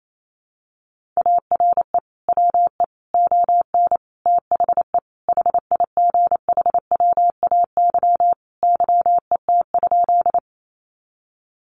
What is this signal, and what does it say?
Morse code, heard as a keyed tone, about 28 wpm, 700 Hz. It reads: ARE WE ON THE HIGHWAY YET?
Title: Sentences from Top 1400 Words 28wpm